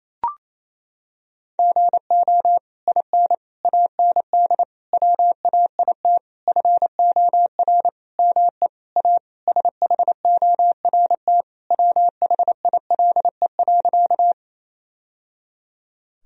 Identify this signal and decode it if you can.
Morse code, heard as a keyed tone, about 28 wpm, 700 Hz. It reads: GO IN AND WAIT FOR ME A SHORT WHILE.
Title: Sentences from Top 1000 Words - Review Part 2 - 28wpm